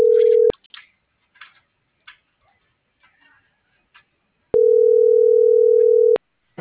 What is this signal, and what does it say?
Ringback tone (North America), 440 + 480 Hz.